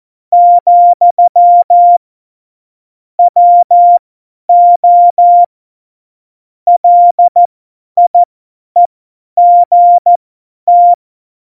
Morse code, heard as a keyed tone, about 14 wpm, spaced as for 10 wpm, 700 Hz. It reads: , WO LIEGT
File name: Morsekurs / Witze / Witz 13 March 2018, 50 F70